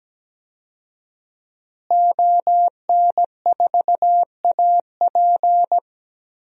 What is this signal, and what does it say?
Morse code, heard as a keyed tone, about 17 wpm, 700 Hz. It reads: ON4AP
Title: DXCC-100 Call Signs - 5 Characters - Repeated 2x - 17wpm